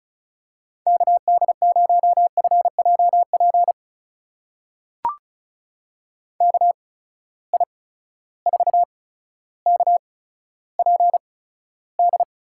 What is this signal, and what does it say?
Morse code, heard as a keyed tone, about 36 wpm, spaced as for 15 wpm, 700 Hz. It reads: KD0FJP KI4KPD
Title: USA Call Signs - 6 Characters - 8x Character Spacing - 35wpm